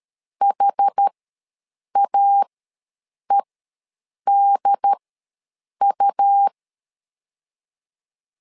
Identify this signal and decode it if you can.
Morse code, heard as a keyed tone, about 13 wpm, spaced as for 7 wpm, 800 Hz. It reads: HAEDU
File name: LEZIONI / 4 Q ws 12wpm 1912